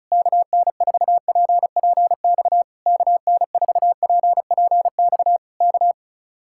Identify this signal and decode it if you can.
Morse code, heard as a keyed tone, about 32 wpm, 700 Hz. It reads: KN4PPX KN4PPX K
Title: Speed-Racing Common QSO Phrases Encore - 35wpm